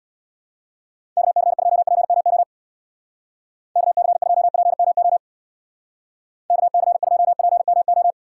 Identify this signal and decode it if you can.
Morse code, heard as a keyed tone, about 59 wpm, 700 Hz. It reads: KC2PMO KC2PMO KC2PMO
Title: Speed Racing - USA 2x3 Call Signs 50wpm